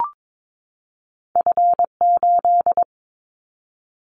Morse code, heard as a keyed tone, about 22 wpm, 700 Hz. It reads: F8
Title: Callsign Prefixes for Top 25 DXCC - 22wpm